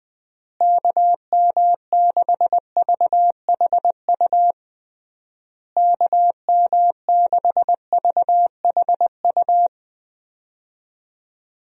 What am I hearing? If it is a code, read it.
Morse code, heard as a keyed tone, about 21 wpm, 700 Hz. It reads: KM6VHU KM6VHU
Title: USA Call Signs - 6 Characters - Repeated 3x - 20wpm